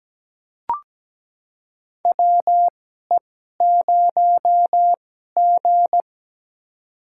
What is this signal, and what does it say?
Morse code, heard as a keyed tone, about 17 wpm, spaced as for 12 wpm, 700 Hz. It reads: WE0G
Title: USA Call Signs - 4 Characters - 2x Character Spacing - 17wpm